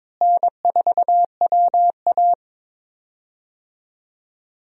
Morse code, heard as a keyed tone, about 22 wpm, 700 Hz. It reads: N4WA
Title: USA Call Signs - 4 Characters - Repeated 3x - 22wpm